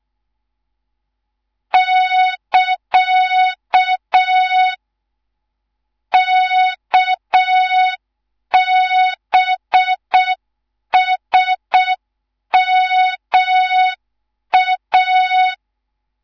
Morse code, heard as a keyed tone, about 6 wpm, 3700 Hz. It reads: <KA> KBSMA
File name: Morze-HA2MN / hang / 11 GYAKORLAT